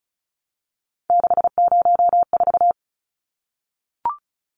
Morse code, heard as a keyed tone, about 34 wpm, 700 Hz. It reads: 604
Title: Numbers - 3-Digit 35wpm